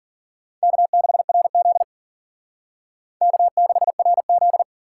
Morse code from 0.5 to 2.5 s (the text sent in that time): K6RZ